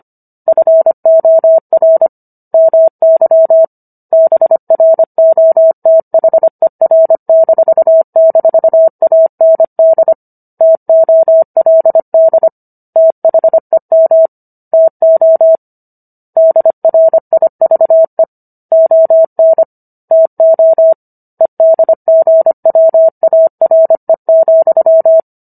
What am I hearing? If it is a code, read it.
Morse code, heard as a keyed tone, about 25 wpm, 650 Hz. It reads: FOR MY BROTHER--AND TOLD THEM TO DRIVE ON TO EDGWARE,